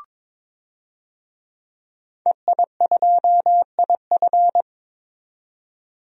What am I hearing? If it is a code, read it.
Morse code, heard as a keyed tone, about 22 wpm, 700 Hz. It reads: EI2IF
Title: DXCC-100 Call Signs - 5 Characters - 22wpm